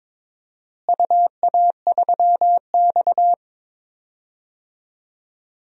Morse code, heard as a keyed tone, about 22 wpm, 700 Hz. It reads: UA3X
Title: DXCC-100 Call Signs - 4 Characters - Repeated 2x - 22wpm